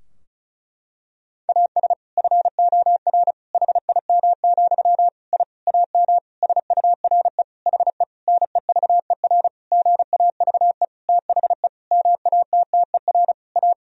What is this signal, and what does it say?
Morse code, heard as a keyed tone, about 36 wpm, 700 Hz. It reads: AS FOR HIM, I AM SURE HE NEVER GAVE THE MATTER A
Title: Sentences from Top 1000 Words - Review Part 2 - 35wpm